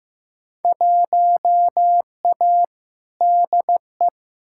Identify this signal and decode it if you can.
Morse code, heard as a keyed tone, about 15 wpm, 700 Hz. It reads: 1A DE
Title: Field Day Exchanges Encore - 15wpm